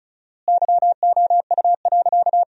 Morse code, heard as a keyed tone, about 36 wpm, 700 Hz. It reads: YOU.
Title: 4x Word-Spacing - Sentences from Top 400 Words - 35wpm